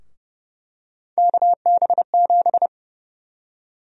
Morse code, heard as a keyed tone, about 32 wpm, 700 Hz. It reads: KB7
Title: USA Call Sign Prefixes - Repeated 2x - 30wpm